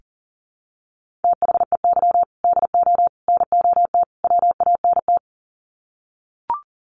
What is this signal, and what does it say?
Morse code, heard as a keyed tone, about 40 wpm, 700 Hz. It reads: THEY DO NOT WANT
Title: Sets of 4 Words 40wpm